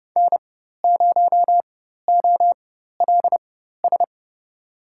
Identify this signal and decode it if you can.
Morse code, heard as a keyed tone, about 32 wpm, spaced as for 14 wpm, 700 Hz. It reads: N0OLS
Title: USA Call Signs - 5 Characters - 4x Character Spacing - 30wpm